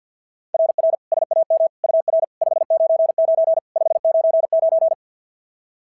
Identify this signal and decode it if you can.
Morse code, heard as a keyed tone, about 53 wpm, 690 Hz. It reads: RR SAM UR 599 599
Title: SOTA QSO Practice 50wpm